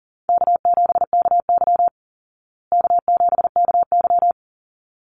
Morse code, heard as a keyed tone, about 40 wpm, 700 Hz. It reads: K7KY K7KY
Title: Call Signs Repeated 3x - 40wpm